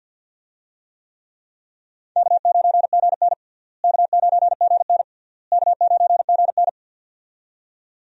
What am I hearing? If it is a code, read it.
Morse code, heard as a keyed tone, about 51 wpm, 700 Hz. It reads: K9GN K9GN K9GN
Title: USA Call Signs - 4 Characters - Repeated 3x - 50wpm